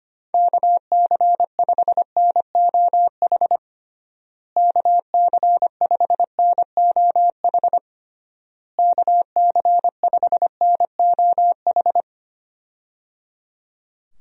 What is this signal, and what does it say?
Morse code, heard as a keyed tone, about 25 wpm, 700 Hz. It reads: KC5NOH KC5NOH KC5NOH
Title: USA Call Signs - 6 Characters - Repeated 3x - 25wpm